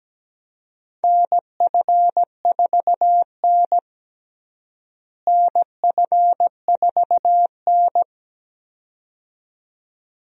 Morse code, heard as a keyed tone, about 17 wpm, 700 Hz. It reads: NF4N NF4N
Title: USA Call Signs - 4 Characters - Repeated 2x - 17wpm